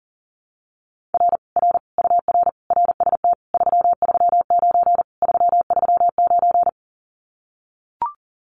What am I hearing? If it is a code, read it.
Morse code, heard as a keyed tone, about 40 wpm, 700 Hz. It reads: R R UR RST 339 339
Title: Common QSO Phrases Encore 40wpm